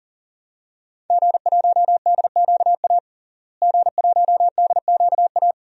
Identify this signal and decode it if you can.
Morse code, heard as a keyed tone, about 40 wpm, 700 Hz. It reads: G1DQA G1DQA
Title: DXCC-100 Call Signs - 5 Characters - Repeated 2x - 40wpm